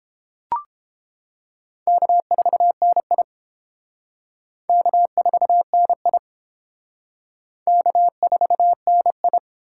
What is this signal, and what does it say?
Morse code, heard as a keyed tone, about 29 wpm, 700 Hz. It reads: K4NI K4NI K4NI